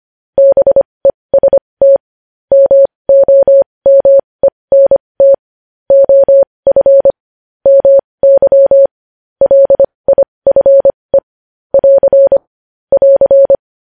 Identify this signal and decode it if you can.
Morse code, heard as a keyed tone, about 25 wpm, spaced as for 20 wpm, 550 Hz. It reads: BEST MOMENT OF MY LIFE <AR> <AR>
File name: podcast / News headlines-1x-25x20